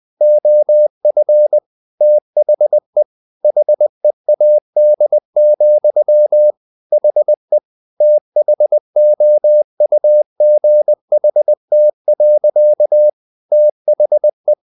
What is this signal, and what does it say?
Morse code, heard as a keyed tone, about 20 wpm, 600 Hz. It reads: OF THE HEAD, HE THOUGHT. THE
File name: the old man and the sea part 2 20wpm